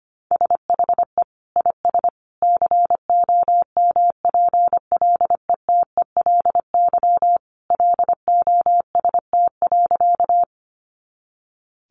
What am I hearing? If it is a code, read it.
Morse code, heard as a keyed tone, about 25 wpm, 700 Hz. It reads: SHE IS COMPLETELY LOST.
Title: Additional Easy Phrases 25wpm